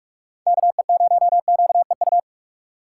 Morse code, heard as a keyed tone, about 45 wpm, 700 Hz. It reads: KE0QEU
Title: USA Call Signs - 6 Characters - 4x Character Spacing - 45wpm